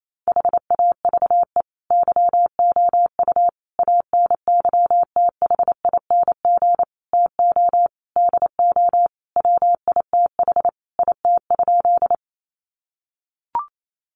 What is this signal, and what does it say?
Morse code, heard as a keyed tone, about 28 wpm, 700 Hz. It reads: HAVE YOU ANYTHING TO DO WITH IT?
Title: Sentences from Top 1000 Words 28wpm